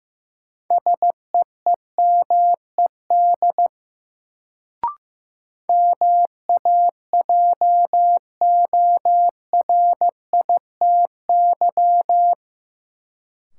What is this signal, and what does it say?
Morse code, heard as a keyed tone, about 15 wpm, 700 Hz. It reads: SEEMED MAJORITY
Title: Sets of 3 Words - New Word Drill - 15wpm